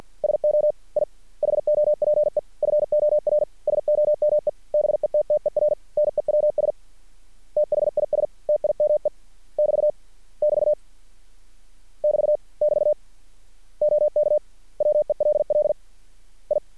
Morse code, heard as a keyed tone, about 50 wpm, 600 Hz. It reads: SO I HOPE FOR SOME BETTER NEWS THIS TIME <BT> <BT> <BT> <BT> OK WELL I
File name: archived / CRSnet